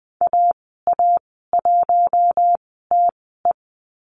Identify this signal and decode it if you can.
Morse code, heard as a keyed tone, about 20 wpm, spaced as for 14 wpm, 700 Hz. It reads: AA1TE
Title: 042 Lesson 8g - Number 1 - Call Signs 20wpm